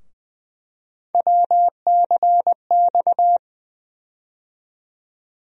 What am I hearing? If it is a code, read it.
Morse code, heard as a keyed tone, about 20 wpm, 700 Hz. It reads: WCX
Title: USA Call Sign Postfixes - 3 Letters 20wpm